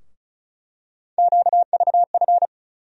Morse code, heard as a keyed tone, about 34 wpm, 700 Hz. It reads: QVF
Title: USA Call Sign Postfixes - 3 Letters 35wpm